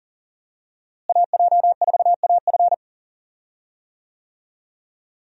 Morse code, heard as a keyed tone, about 40 wpm, 700 Hz. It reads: AJ4AF
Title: USA Call Signs - 5 Characters - Repeated 3x - 40wpm